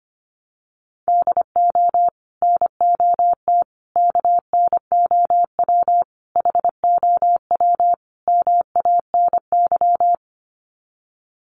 Morse code, heard as a keyed tone, about 25 wpm, 700 Hz. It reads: DO NOT KNOW HOW MANY
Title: Sets of 5 Words Encore 25wpm